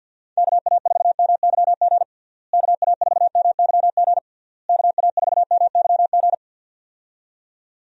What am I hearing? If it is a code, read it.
Morse code, heard as a keyed tone, about 48 wpm, 700 Hz. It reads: KA4MYG KA4MYG KA4MYG
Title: USA Call Signs - 6 Characters - Repeated 3x - 50wpm